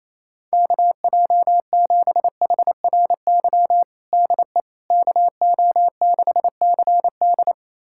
Morse code, heard as a keyed tone, about 28 wpm, 700 Hz. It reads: KJ7HRY DE KO6CD
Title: USA Callsign DE Callsign 28wpm